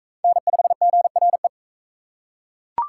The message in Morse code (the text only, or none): N5GRE